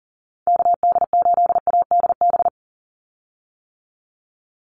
Morse code, heard as a keyed tone, about 40 wpm, 700 Hz. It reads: KD8ADB
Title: USA Call Signs 40wpm